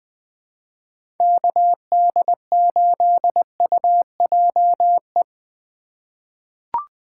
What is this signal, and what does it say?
Morse code, heard as a keyed tone, about 21 wpm, 700 Hz. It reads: KD8UJE
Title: USA Call Signs - 4x Character Spacing - 20wpm